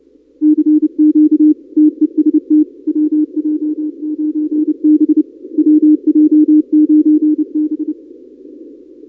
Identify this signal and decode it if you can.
Morse code, heard as a keyed tone, about 28 wpm, 310 Hz. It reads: CQ TEST WJ9B WJ9B
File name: contests / 2019-cw-wpx-kt5j-k5pi- -2330 0-31